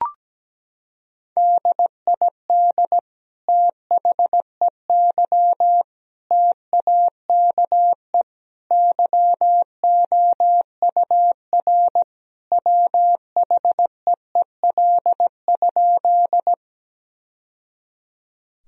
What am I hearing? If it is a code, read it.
Morse code, heard as a keyed tone, about 17 wpm, 700 Hz. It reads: DID THEY TAKE YOUR WHEEL?
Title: Sentences from Top 1000 Words - Review Part 3 - 17wpm